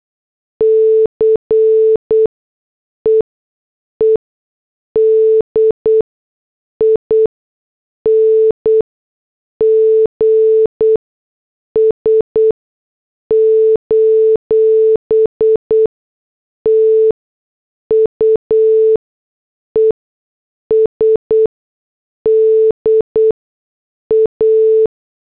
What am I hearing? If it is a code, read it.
Morse code, heard as a keyed tone, about 8 wpm, spaced as for 6 wpm, 440 Hz. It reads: CEEDINGS: TUESDA